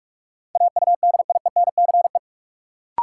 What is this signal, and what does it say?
Morse code, heard as a keyed tone, about 46 wpm, 700 Hz. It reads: AUDIENCE
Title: Sentences from Top 800 Words - New Word Drill - 45wpm